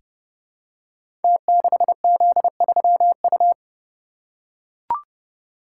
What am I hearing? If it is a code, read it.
Morse code, heard as a keyed tone, about 30 wpm, 700 Hz. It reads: T6Z3U
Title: Code Groups - 5 Letter:Numbers - 30wpm